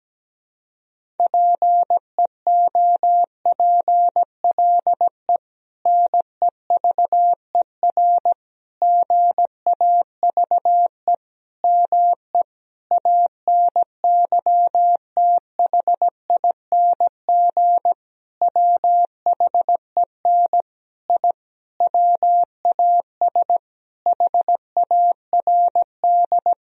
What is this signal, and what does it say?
Morse code, heard as a keyed tone, about 17 wpm, 700 Hz. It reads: PEOPLE NEVER GAVE ME ANYTHING WHEN I WAS HARD
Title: Sentences from Top 1000 Words - Review Part 1 - 17wpm